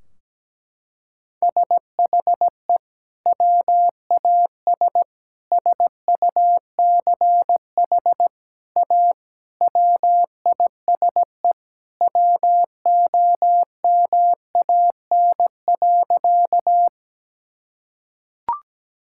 Morse code, heard as a keyed tone, about 17 wpm, 700 Hz. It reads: SHE WAS SUCH A WISE WOMAN.
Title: Sentences from Top 1600 Words 17wpm